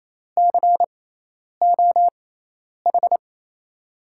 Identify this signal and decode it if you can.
Morse code, heard as a keyed tone, about 28 wpm, spaced as for 10 wpm, 700 Hz. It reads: COH
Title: USA Call Signs - 6 Characters - 6x Character Spacing - 28wpm